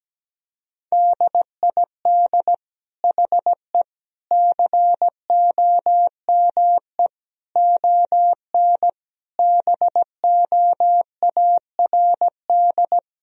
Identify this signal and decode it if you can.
Morse code, heard as a keyed tone, about 17 wpm, 700 Hz. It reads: DID HE COME ON BOARD
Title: Sentences from Top 1000 Words - Review Part 4 - 17wpm